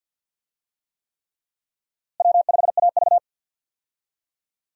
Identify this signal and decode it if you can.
Morse code, heard as a keyed tone, about 47 wpm, 700 Hz. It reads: W5AV
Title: USA Call Signs - 4 Characters - Repeated 3x - 50wpm